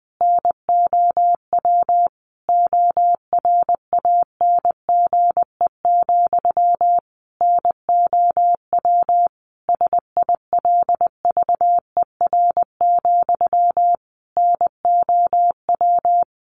Morse code, heard as a keyed tone, about 20 wpm, 700 Hz. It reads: NOW ORANGE, NOW SILVER, NOW
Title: A Sound Of Thunder 20wpm